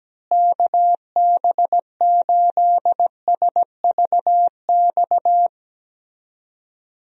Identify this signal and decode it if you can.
Morse code, heard as a keyed tone, about 17 wpm, 700 Hz. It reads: KB8SVX
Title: USA Call Signs - 6 Characters - Repeated 2x - 17wpm